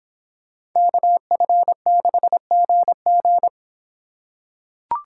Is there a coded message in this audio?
Yes: Morse code from 0.8 to 3.5 s, heard as a keyed tone, about 25 wpm, 700 Hz. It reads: KF6GG